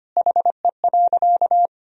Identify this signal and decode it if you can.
Morse code, heard as a keyed tone, about 25 wpm, 700 Hz. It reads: HE.
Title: Sentences from Top 100 Words and Abbreviations 25wpm